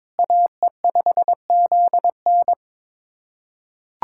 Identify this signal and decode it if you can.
Morse code, heard as a keyed tone, about 21 wpm, 700 Hz. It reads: AE5ZN
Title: USA Callsign DE Callsign 22wpm